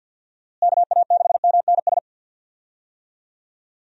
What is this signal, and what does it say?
Morse code, heard as a keyed tone, about 50 wpm, 700 Hz. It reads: KA6MNS
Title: USA Call Signs - 6 Characters - Repeated 3x - 50wpm